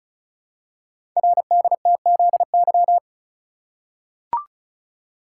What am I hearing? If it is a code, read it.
Morse code, heard as a keyed tone, about 36 wpm, 700 Hz. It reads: RDTZY
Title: Code Groups - 5 Letters - 35wpm